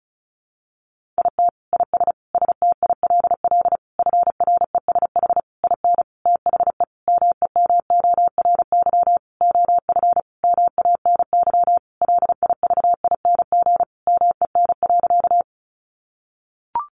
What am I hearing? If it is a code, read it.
Morse code, heard as a keyed tone, about 34 wpm, 700 Hz. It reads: IT IS STILL FRESH IN THE MEMORY OF MANY LIVING MEN.
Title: Sentences from Top 1300 Words 35wpm